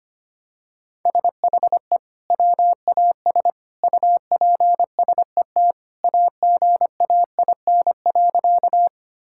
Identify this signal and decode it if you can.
Morse code, heard as a keyed tone, about 25 wpm, 700 Hz. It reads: SHE WAS UPSET AGAIN.
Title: Sentences from Top 1000 Words - Review Part 5 - 25wpm